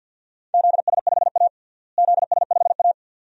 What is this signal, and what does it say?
Morse code, heard as a keyed tone, about 51 wpm, 700 Hz. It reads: ZS5A ZS5A
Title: DXCC-100 Call Signs - 4 Characters - Repeated 2x - 50wpm